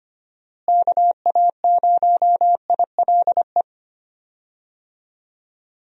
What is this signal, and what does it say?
Morse code, heard as a keyed tone, about 25 wpm, 700 Hz. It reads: KA0ILE
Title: USA Call Signs - 6 Characters - Repeated 2x - 25wpm